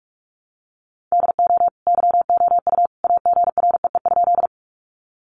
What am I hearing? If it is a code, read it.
Morse code, heard as a keyed tone, about 44 wpm, 700 Hz. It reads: DO YOU AGREE?
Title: Sentences from Top 1100 Words 45wpm